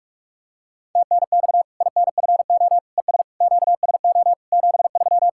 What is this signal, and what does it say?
Morse code, heard as a keyed tone, about 44 wpm, 700 Hz. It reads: TNX INFO ES QSO 73
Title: Practice QSO - Exchange 4 - Ending - 45wpm